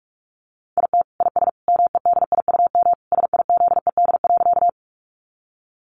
Morse code, heard as a keyed tone, about 44 wpm, 700 Hz. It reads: IT IS MEDIUM SIZED.
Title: Easy Phrases 45wpm